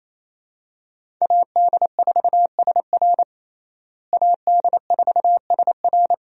Morse code, heard as a keyed tone, about 28 wpm, 700 Hz. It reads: AD4SR AD4SR
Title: USA Call Signs - 5 Characters - Repeated 2x - 28wpm